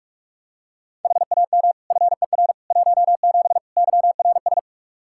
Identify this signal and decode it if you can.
Morse code, heard as a keyed tone, about 46 wpm, 700 Hz. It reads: HAM FER 17 YRS
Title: Practice QSO - Exchange 3 - More Info - 45wpm